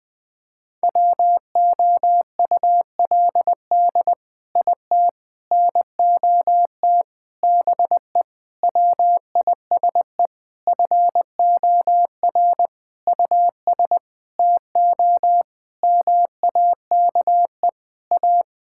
Morse code, heard as a keyed tone, about 21 wpm, 700 Hz. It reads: WOULD IT NOT BE WISE FOR US TO MAKE A
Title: Sentences from Top 1600 Words 20wpm